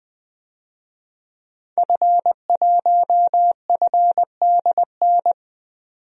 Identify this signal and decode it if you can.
Morse code, heard as a keyed tone, about 20 wpm, 700 Hz. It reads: F1FDN
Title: DXCC-100 Call Signs - 5 Characters - 20wpm